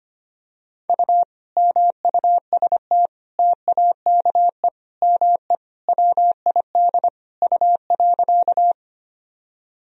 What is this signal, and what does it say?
Morse code, heard as a keyed tone, about 25 wpm, 700 Hz. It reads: U MUST TAKE ME WID U.
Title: Sentences from Top 200 Words and Abbreviations 25wpm